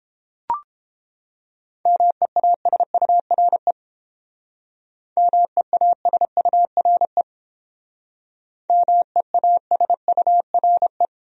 Morse code, heard as a keyed tone, about 28 wpm, 700 Hz. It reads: MEASURE MEASURE MEASURE